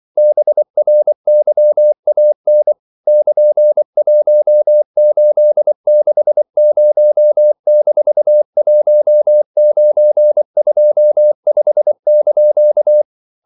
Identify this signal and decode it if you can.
Morse code, heard as a keyed tone, about 24 wpm, 600 Hz. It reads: BRYAN <KN>1860-1925)
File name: CW / Practice / EnglishQuotations0324